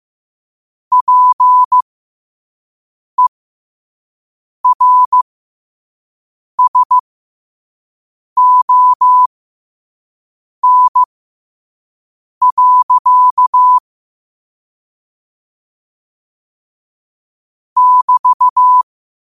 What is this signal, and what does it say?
Morse code, heard as a keyed tone, about 15 wpm, spaced as for 5 wpm, 1000 Hz. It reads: PERSON. <BT>
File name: qotdfiles / 5wpm